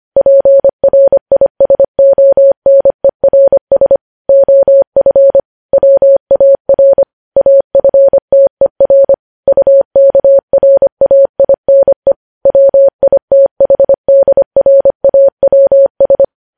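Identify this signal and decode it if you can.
Morse code, heard as a keyed tone, about 25 wpm, 550 Hz. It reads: PRISONERS OF WAR AFTER UKRAINE WITHDRAWS
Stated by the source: Morse code practice of news headlines sent at 25 wpm.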